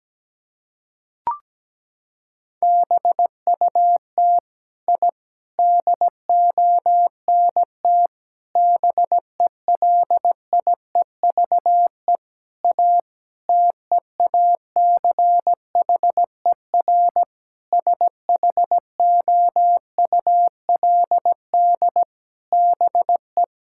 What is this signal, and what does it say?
Morse code, heard as a keyed tone, about 17 wpm, 700 Hz. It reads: BUT I DONT BELIEVE A TEACHER SHOULD BE
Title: Sentences from Top 1000 Words - Review Part 6 - 17wpm